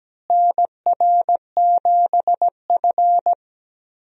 Morse code, heard as a keyed tone, about 17 wpm, 700 Hz. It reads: NR7F
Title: USA Call Signs - 4 Characters - Repeated 2x - 17wpm